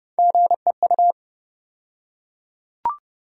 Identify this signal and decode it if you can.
Morse code, heard as a keyed tone, about 30 wpm, 700 Hz. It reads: GEU
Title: USA Call Sign Postfixes - 3 Letters 30wpm